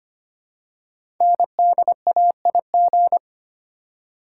Morse code, heard as a keyed tone, about 25 wpm, 700 Hz. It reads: NDAIG
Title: Code Groups - 5 Letters - 25wpm